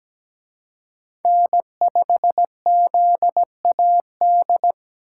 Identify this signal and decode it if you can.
Morse code, heard as a keyed tone, about 20 wpm, 700 Hz. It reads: N5ZAD
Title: Speed Racing - USA 1x3 Call Signs 17wpm